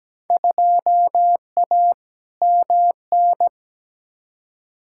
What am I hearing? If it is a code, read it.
Morse code, heard as a keyed tone, about 17 wpm, 700 Hz. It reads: 2A MN
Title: Field Day Exchanges Encore - 17wpm